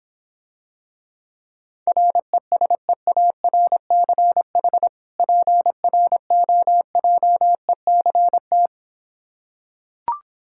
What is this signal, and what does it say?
Morse code, heard as a keyed tone, about 26 wpm, 700 Hz. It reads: RESEARCH PROJECT